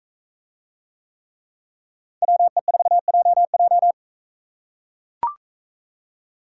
Morse code, heard as a keyed tone, about 43 wpm, 700 Hz. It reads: WE4JJ